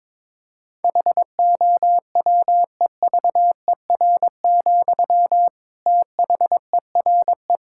Morse code, heard as a keyed tone, about 22 wpm, 700 Hz. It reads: HOWEVER, THERE
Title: Sentences from Top 1600 Words 22wpm